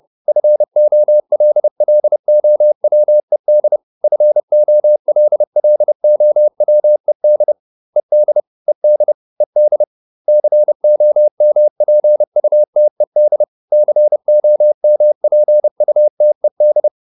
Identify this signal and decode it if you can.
Morse code, heard as a keyed tone, about 30 wpm, 600 Hz. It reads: FOLLOWED FOLLOWED ED ED ED COMPUTED COMPUTED